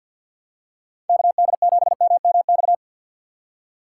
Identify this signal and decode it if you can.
Morse code, heard as a keyed tone, about 52 wpm, 700 Hz. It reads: KD7MMX
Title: USA Call Signs - 2x Character Spacing - 50wpm